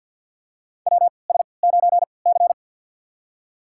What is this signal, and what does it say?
Morse code, heard as a keyed tone, about 52 wpm, spaced as for 29 wpm, 700 Hz. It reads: WS9C
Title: USA Call Signs - 4 Characters - 3x Character Spacing - 50wpm